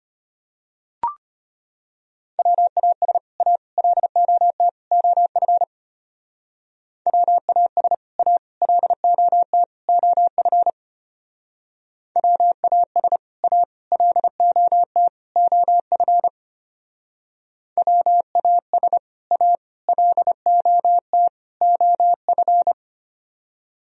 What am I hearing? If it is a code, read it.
Morse code, heard as a keyed tone, about 32 wpm, 700 Hz. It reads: WAS A LOT OF WAS A LOT OF WAS A LOT OF WAS A LOT OF